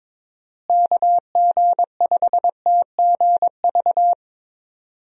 Morse code, heard as a keyed tone, about 21 wpm, 700 Hz. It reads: KG5TGV
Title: USA Call Signs - 6 Characters - Repeated 3x - 22wpm